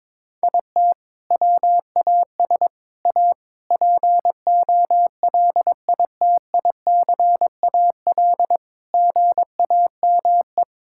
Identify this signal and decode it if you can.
Morse code, heard as a keyed tone, about 22 wpm, 700 Hz. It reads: IT WAS A POLITICAL GAME
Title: Sentences from Top 1000 Words - Review Part 5 - 22wpm